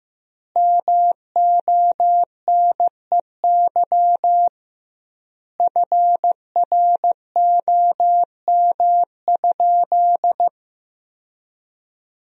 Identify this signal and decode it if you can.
Morse code, heard as a keyed tone, about 15 wpm, 700 Hz. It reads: MONEY FROM?
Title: 2x Word-Spacing - Sentences from Top 400 Words - 15wpm